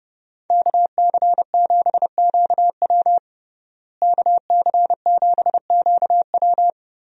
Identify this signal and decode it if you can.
Morse code, heard as a keyed tone, about 30 wpm, 700 Hz. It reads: KC7QW KC7QW
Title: USA Call Signs - 5 Characters - Repeated 3x - 30wpm